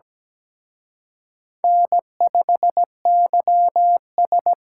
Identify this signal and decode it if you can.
Morse code, heard as a keyed tone, about 20 wpm, 700 Hz. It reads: N5YS
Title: Speed Racing - USA Call Signs 17wpm